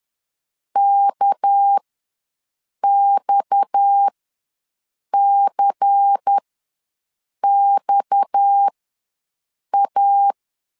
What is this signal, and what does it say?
Morse code, heard as a keyed tone, about 10 wpm, spaced as for 6 wpm, 800 Hz. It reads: KXCXA
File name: LEZIONI / 6 PKX0 / 6 PKX ws 10wpm 3706